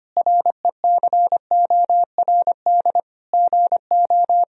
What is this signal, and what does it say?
Morse code, heard as a keyed tone, about 25 wpm, 700 Hz. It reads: RECORD GO
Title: Sentences from Top 1000 Words - Review Part 4 - 25wpm